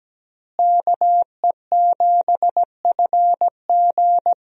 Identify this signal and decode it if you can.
Morse code, heard as a keyed tone, about 17 wpm, 700 Hz. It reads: KE7FG
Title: USA Call Signs - 5 Characters - Repeated 2x - 17wpm